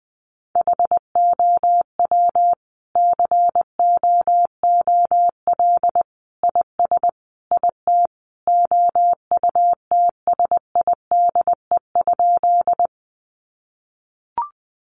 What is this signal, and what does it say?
Morse code, heard as a keyed tone, about 20 wpm, 700 Hz. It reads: HOW COOL IS IT OUTSIDE?
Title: Sentences from Top 1200 Words 20wpm